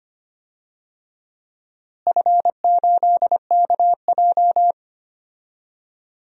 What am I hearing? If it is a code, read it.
Morse code, heard as a keyed tone, about 25 wpm, 700 Hz. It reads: F8KJ
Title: DXCC-100 Call Signs - 4 Characters - Repeated 2x - 25wpm